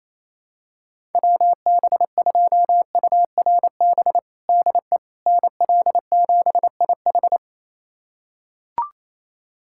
Morse code, heard as a keyed tone, about 28 wpm, 700 Hz. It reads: WB2URB DE NL7IH
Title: USA Callsign DE Callsign 28wpm